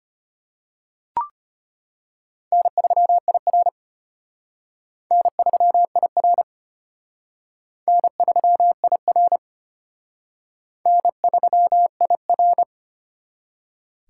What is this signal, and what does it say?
Morse code, heard as a keyed tone, about 31 wpm, 700 Hz. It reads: N3IR N3IR N3IR N3IR